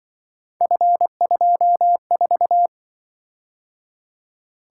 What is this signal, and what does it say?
Morse code, heard as a keyed tone, about 24 wpm, 700 Hz. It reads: F24
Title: Speed-Racing - Three Letter-Number Combinations - 20wpm